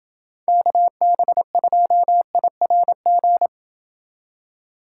Morse code, heard as a keyed tone, about 28 wpm, 700 Hz. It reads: KB2IRG